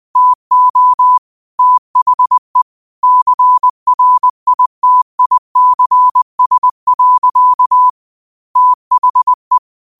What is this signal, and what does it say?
Morse code, heard as a keyed tone, about 20 wpm, 1000 Hz. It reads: TO THE CRITICS. THE
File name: qotdfiles / 20wpm